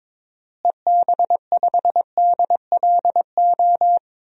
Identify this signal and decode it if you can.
Morse code, heard as a keyed tone, about 22 wpm, 700 Hz. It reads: EB5DLO
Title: DXCC-100 Call Signs - 6 Characters - Repeated 2x - 22wpm